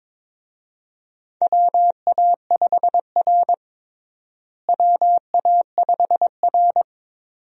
Morse code, heard as a keyed tone, about 22 wpm, 700 Hz. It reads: WA5R WA5R
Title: USA Call Signs - 4 Characters - Repeated 2x - 22wpm